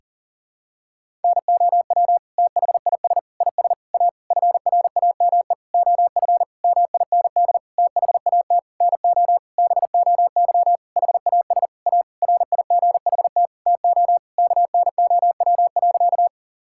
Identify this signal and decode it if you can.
Morse code, heard as a keyed tone, about 40 wpm, 700 Hz. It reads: NOW THIS IS A FRAME OF MIND THAT NO BOY HAS A RIGHT TO KNOW.
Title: Sentences from Top 1000 Words - Review Part 6 - 40wpm